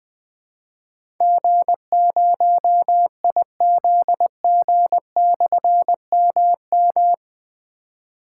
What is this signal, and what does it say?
Morse code, heard as a keyed tone, about 20 wpm, 700 Hz. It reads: G0IZG/MM
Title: Difficult Call Signs 20wpm